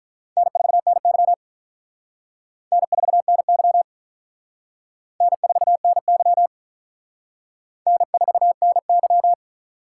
Morse code, heard as a keyed tone, about 41 wpm, 700 Hz. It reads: N4NY N4NY N4NY N4NY
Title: Speed Racing - USA 1x2 Call Signs 35wpm